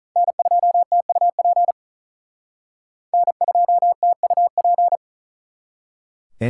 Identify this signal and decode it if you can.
Morse code, heard as a keyed tone, about 43 wpm, 700 Hz. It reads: N2TUP N2TUP
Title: Speed Racing - USA 1x3 Call Signs 35wpm